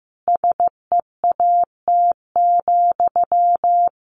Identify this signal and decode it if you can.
Morse code, heard as a keyed tone, about 15 wpm, 700 Hz. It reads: SEAT,
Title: Sentences from Top 1500 Words 15wpm